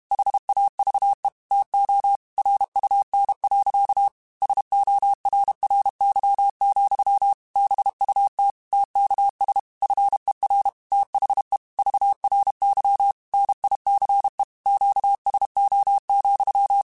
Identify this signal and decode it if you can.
Morse code, heard as a keyed tone, about 32 wpm, 800 Hz. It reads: HAVE TO RUN. SORRY, BUT TKS FER THE VRY NICE QSO,
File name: QSOs / QSOs-mp3s / kc7sac@30